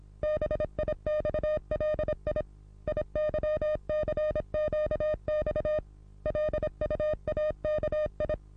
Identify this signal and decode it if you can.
Morse code, heard as a keyed tone, about 26 wpm, 610 Hz. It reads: BIXLI IYCQX LUAKI
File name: repos / Morsekurs / DJ4UF Morsen27